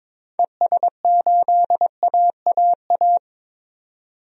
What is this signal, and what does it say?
Morse code, heard as a keyed tone, about 22 wpm, 700 Hz. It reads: ES8AAA
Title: DXCC-100 Call Signs - 6 Characters - 22wpm